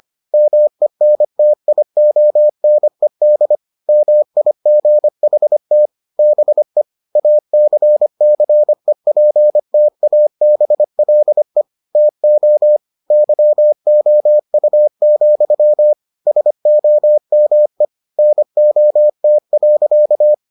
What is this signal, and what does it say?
Morse code, heard as a keyed tone, about 25 wpm, 600 Hz. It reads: MENTIONED MIGHT BE ACCEPTABLE TO YOU, SOME NOT.